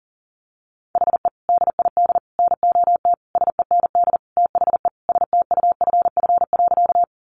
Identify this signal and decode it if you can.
Morse code, heard as a keyed tone, about 37 wpm, 700 Hz. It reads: HE DID NOT SEND THE STUFF.
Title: Sentences from Top 1500 Words 40wpm